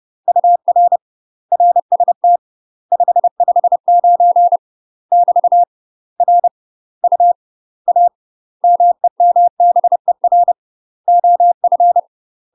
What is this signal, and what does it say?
Morse code, heard as a keyed tone, about 30 wpm, 700 Hz. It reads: UR RST 559 <BT> R U A MEMBER OF
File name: UK MASTER QSOs TRAINING / 30WPM / UK MASTER QSOs 0013